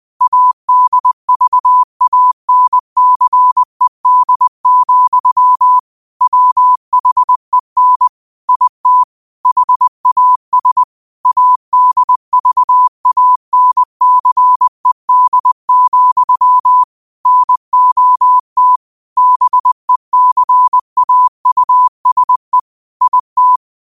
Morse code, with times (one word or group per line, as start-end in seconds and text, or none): 0.2-5.8 ADVANCED,
6.2-8.1 WHEN
8.5-9.0 IT
9.5-10.8 HAS
11.3-16.8 ADVANCED,
17.3-18.8 NOT
19.2-22.6 BECAUSE
23.0-23.6 IT